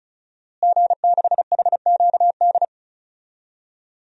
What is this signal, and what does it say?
Morse code, heard as a keyed tone, about 36 wpm, 700 Hz. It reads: G6HQD
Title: DXCC-100 Call Signs - 5 Characters - Repeated 3x - 35wpm